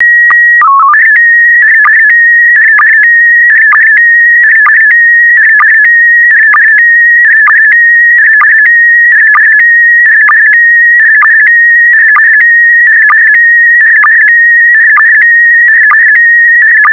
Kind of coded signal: SSTV (slow-scan television)